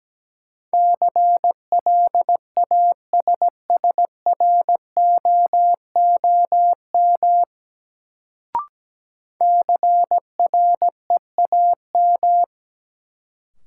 Morse code, heard as a keyed tone, about 17 wpm, 700 Hz. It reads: CLASSROOM CREAM
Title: Sentences from Top 1500 Words - New Word Drill Alt - 17wpm